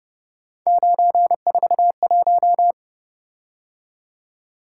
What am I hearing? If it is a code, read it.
Morse code, heard as a keyed tone, about 30 wpm, 700 Hz. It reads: 941
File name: Speed-Racing - Three Letter-Number Combinations - 25wpm